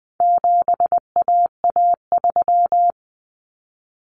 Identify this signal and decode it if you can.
Morse code, heard as a keyed tone, about 19 wpm, 700 Hz. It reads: 7AA3
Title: Common Cut Numbers - 4-Digit 20wpm